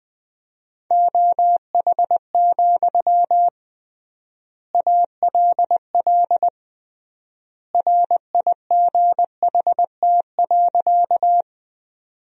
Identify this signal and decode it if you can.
Morse code, heard as a keyed tone, about 20 wpm, 700 Hz. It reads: OH, ALL RIGHT.
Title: 3x Word-Spacing - Sentences from Top 500 Words - 20wpm